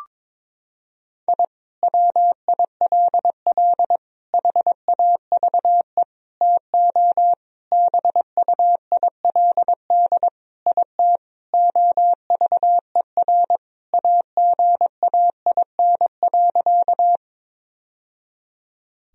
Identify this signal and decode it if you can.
Morse code, heard as a keyed tone, about 22 wpm, 700 Hz. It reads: I WILL HAVE TO BUILD IT OVER AGAIN.
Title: Sentences from Top 1000 Words - Review Part 1 - 22wpm